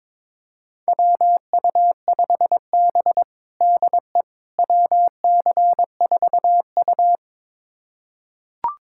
Morse code, heard as a keyed tone, about 22 wpm, 700 Hz. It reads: WU5B DE WC4U
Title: USA Callsign DE Callsign - 4 Characters - 22wpm